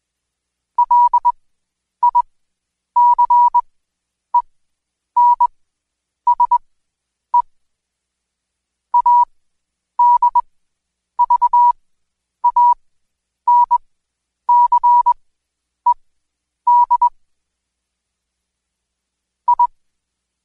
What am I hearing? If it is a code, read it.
Morse code, heard as a keyed tone, about 20 wpm, spaced as for 9 wpm, 970 Hz. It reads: LICENSE ADVANCED I
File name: Code Quick / AudioDownld2 / Seg116-Day21-DayDrill 21